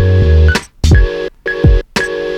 There is no code.